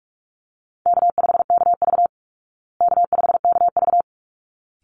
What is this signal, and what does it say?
Morse code, heard as a keyed tone, about 44 wpm, 700 Hz. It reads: K5KV K5KV
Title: Call Signs Repeated 3x - 45wpm